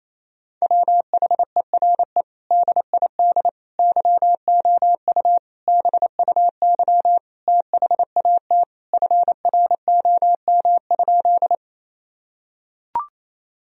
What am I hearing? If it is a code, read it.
Morse code, heard as a keyed tone, about 28 wpm, 700 Hz. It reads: WHERE DID YOU BUY THAT FROM?
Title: Sentences from Top 1000 Words - Review Part 4 - 28wpm